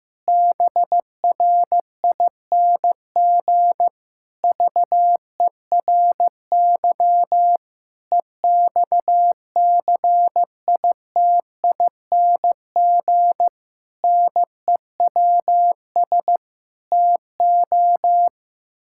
Morse code, heard as a keyed tone, about 15 wpm, 700 Hz. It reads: BRING VERY EXCITING NEWS TO
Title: Pangram Challenge v2 - 15wpm